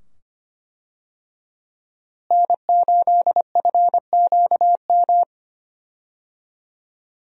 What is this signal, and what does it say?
Morse code, heard as a keyed tone, about 25 wpm, 700 Hz. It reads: N8FQM
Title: USA Call Signs - 5 Characters - Repeated 2x - 25wpm